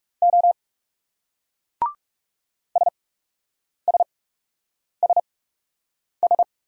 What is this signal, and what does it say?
Morse code, heard as a keyed tone, about 39 wpm, spaced as for 8 wpm, 700 Hz. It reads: O SSSS